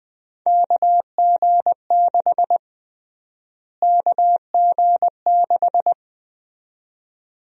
Morse code, heard as a keyed tone, about 20 wpm, 700 Hz. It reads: KG6 KG6
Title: USA Call Sign Prefixes - Repeated 2x - 20wpm